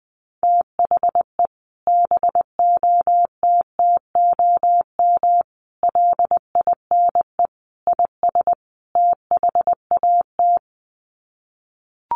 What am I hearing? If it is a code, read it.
Morse code, heard as a keyed tone, about 20 wpm, 700 Hz. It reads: THE BOTTOM LINE IS THAT
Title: Sets of 5 Words Encore 20wpm